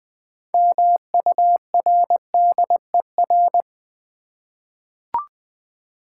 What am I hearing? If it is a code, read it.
Morse code, heard as a keyed tone, about 20 wpm, 700 Hz. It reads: MURDER
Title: 6 Letter Words - First 3 Letters Given - 20wpm